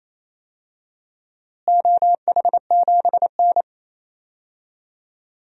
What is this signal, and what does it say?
Morse code, heard as a keyed tone, about 28 wpm, 700 Hz. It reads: OH7N
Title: DXCC-100 Call Signs - 4 Characters - Repeated 3x - 28wpm